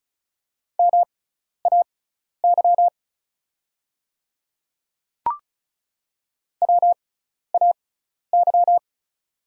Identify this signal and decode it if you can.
Morse code, heard as a keyed tone, about 36 wpm, spaced as for 12 wpm, 700 Hz. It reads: MAY WAY
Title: 6x Character Spacing - Top 300 Words - 35wpm